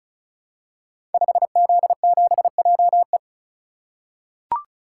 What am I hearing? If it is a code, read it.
Morse code, heard as a keyed tone, about 36 wpm, 700 Hz. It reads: 5Z7JE
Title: Code Groups - 5 Letter:Numbers - 35wpm